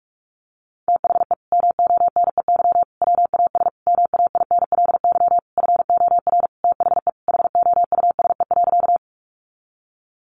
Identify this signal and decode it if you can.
Morse code, heard as a keyed tone, about 44 wpm, 700 Hz. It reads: THE MONEY WAS MAINLY FOR THE HOUSE.
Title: Sentences from Top 1400 Words 45wpm